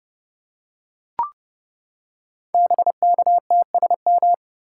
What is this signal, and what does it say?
Morse code, heard as a keyed tone, about 30 wpm, 700 Hz. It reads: BKTSM
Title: Code Groups - 5 Letters - 30wpm